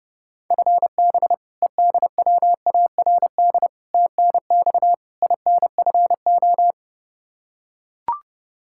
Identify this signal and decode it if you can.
Morse code, heard as a keyed tone, about 30 wpm, 700 Hz. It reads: FB EDWARD TNX INFO
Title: Practice QSO - Exchange 3 - More Info - 30wpm